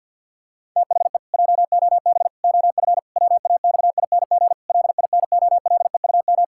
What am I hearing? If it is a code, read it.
Morse code, heard as a keyed tone, about 50 wpm, 700 Hz. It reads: THE JOB OF WAXING LINOLEUM
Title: Pangram Challenge v2 - 50wpm